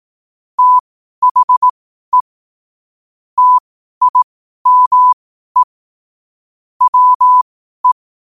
Morse code, heard as a keyed tone, about 18 wpm, spaced as for 12 wpm, 1000 Hz. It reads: THE TIME WE
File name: qotdfiles / 12wpm